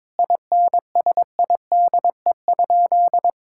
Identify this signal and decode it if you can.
Morse code, heard as a keyed tone, about 22 wpm, 700 Hz. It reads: INSIDE?
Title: Sentences from Top 1000 Words - Review Part 6 - 22wpm